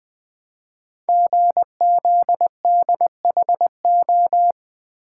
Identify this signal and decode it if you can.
Morse code, heard as a keyed tone, about 20 wpm, 700 Hz. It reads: GZDHO
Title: Code Groups - 5 Letters - 20wpm